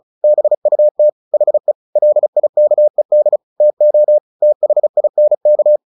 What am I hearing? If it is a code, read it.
Morse code, heard as a keyed tone, about 34 wpm, 600 Hz. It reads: BUT HE LIKED TO THINK
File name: the old man and the sea 2 35wpm